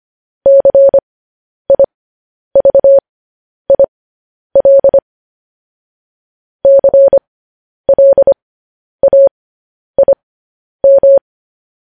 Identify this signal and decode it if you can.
Morse code, heard as a keyed tone, about 25 wpm, spaced as for 10 wpm, 550 Hz. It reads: CIVIL CLAIM
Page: news headlines in morse code at 10 wpm